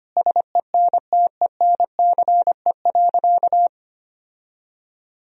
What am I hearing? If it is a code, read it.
Morse code, heard as a keyed tone, about 25 wpm, 700 Hz. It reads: SENTENCE.
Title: Sentences from Top 1000 Words - Review Part 1 - 25wpm